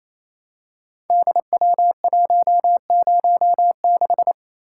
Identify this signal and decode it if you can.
Morse code, heard as a keyed tone, about 27 wpm, 700 Hz. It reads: DW106
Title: Code Groups - 5 Letter:Numbers - 28wpm